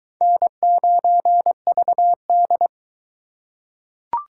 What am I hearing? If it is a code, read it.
Morse code, heard as a keyed tone, about 22 wpm, 700 Hz. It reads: N9VD